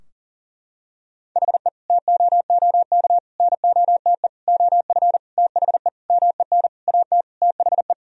Morse code, heard as a keyed tone, about 40 wpm, 700 Hz. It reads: HE TOOK NOTE OF THE MEN AT THE
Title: Sentences from Top 1000 Words - Review Part 2 - 40wpm